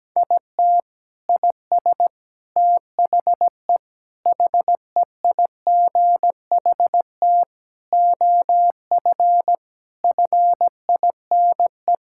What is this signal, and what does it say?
Morse code, heard as a keyed tone, about 17 wpm, 700 Hz. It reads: IT IS THE HEIGHT OF FINE
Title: Sentences from Top 1600 Words 17wpm